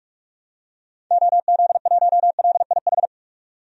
Morse code, heard as a keyed tone, about 47 wpm, 700 Hz. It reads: OZ1LIH
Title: DXCC-100 Call Signs - 6 Characters - 45wpm